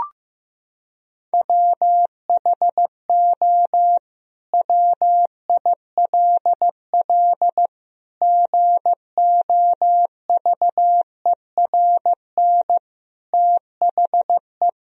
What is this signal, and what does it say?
Morse code, heard as a keyed tone, about 15 wpm, 700 Hz. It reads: WHO WILL GOVERN THE
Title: Sentences from Top 1000 Words - Review Part 2 - 15wpm